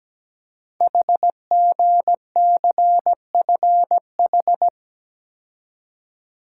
Morse code, heard as a keyed tone, about 17 wpm, 700 Hz. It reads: HGCFH
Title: Code Groups - 5 Letters - 17wpm